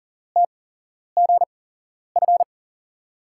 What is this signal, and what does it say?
Morse code, heard as a keyed tone, about 44 wpm, spaced as for 11 wpm, 700 Hz. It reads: TGF
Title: USA Call Signs - 5 Characters - 8x Character Spacing - 40wpm